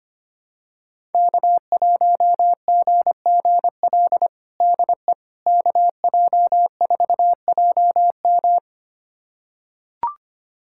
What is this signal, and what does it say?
Morse code, heard as a keyed tone, about 25 wpm, 700 Hz. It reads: K1GGL DE KJ4JM
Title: USA Callsign DE Callsign - 5 Characters - 25wpm